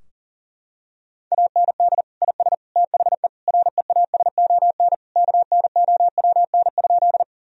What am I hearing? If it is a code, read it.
Morse code, heard as a keyed tone, about 40 wpm, 700 Hz. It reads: AND IS THE REASON KNOWN?
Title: Sentences from Top 1000 Words - Review Part 3 - 40wpm